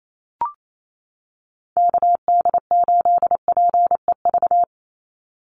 Morse code, heard as a keyed tone, about 28 wpm, 700 Hz. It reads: KD8PEV
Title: USA Call Signs 28wpm